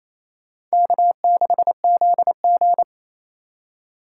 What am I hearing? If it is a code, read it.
Morse code, heard as a keyed tone, about 37 wpm, 700 Hz. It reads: K6ZG